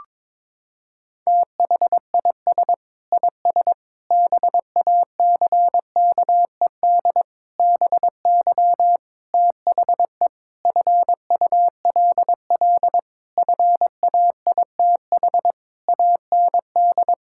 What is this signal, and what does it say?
Morse code, heard as a keyed tone, about 22 wpm, 700 Hz. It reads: THIS IS BACKED BY THE FULL FAITH AND
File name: Sentences with 9-Word Phrases - 22wpm